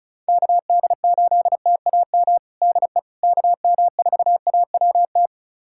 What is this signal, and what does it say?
Morse code, heard as a keyed tone, about 36 wpm, 700 Hz. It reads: KD8TAM DE KM4AWT
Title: USA Callsign DE Callsign - 6 Characters - 35wpm